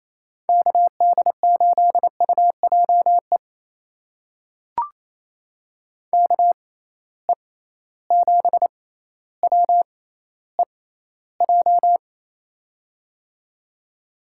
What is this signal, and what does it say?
Morse code, heard as a keyed tone, about 28 wpm, spaced as for 14 wpm, 700 Hz. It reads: KD8UJE KE7WEJ
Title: USA Call Signs - 6 Characters - 6x Character Spacing - 28wpm